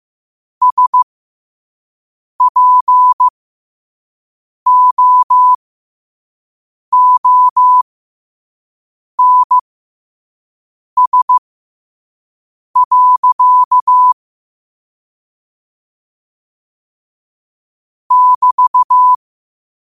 Morse code, heard as a keyed tone, about 15 wpm, spaced as for 5 wpm, 1000 Hz. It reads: SPOONS. <BT>
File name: qotdfiles / 5wpm